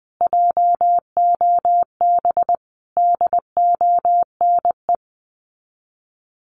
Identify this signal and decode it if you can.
Morse code, heard as a keyed tone, about 20 wpm, 700 Hz. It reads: JOB DONE
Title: Sets of 5 Words Encore2 20wpm